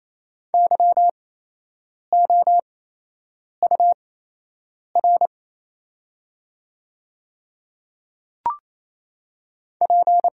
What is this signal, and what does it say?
Morse code, heard as a keyed tone, about 28 wpm, spaced as for 8 wpm, 700 Hz. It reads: YOUR P
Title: 8x Character Spacing - Top 300 Words - 28wpm